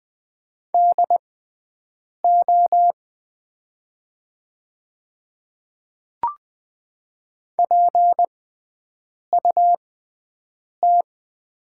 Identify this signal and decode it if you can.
Morse code, heard as a keyed tone, about 20 wpm, spaced as for 7 wpm, 700 Hz. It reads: DO PUT
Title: 6x Character Spacing - Top 200 Words - 20wpm